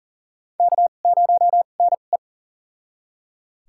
Morse code, heard as a keyed tone, about 37 wpm, spaced as for 28 wpm, 700 Hz. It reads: K0NE
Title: USA Call Signs - 2x Character Spacing - 40wpm